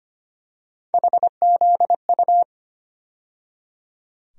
Morse code, heard as a keyed tone, about 25 wpm, 700 Hz. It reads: HZU
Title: USA Call Sign Postfixes - 3 Letters 25wpm